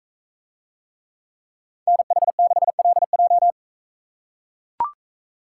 Morse code, heard as a keyed tone, about 44 wpm, 700 Hz. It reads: NH6LJ